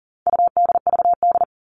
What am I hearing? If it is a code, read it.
Morse code, heard as a keyed tone, about 40 wpm, 700 Hz. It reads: UDVD
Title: All Cut Numbers - 4-Digit 40wpm